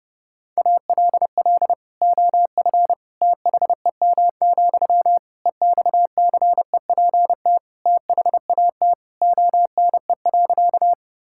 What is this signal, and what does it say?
Morse code, heard as a keyed tone, about 32 wpm, 700 Hz. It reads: ALL OF THEM, EXCEPT THAT ONE.
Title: Sentences from Top 1600 Words 30wpm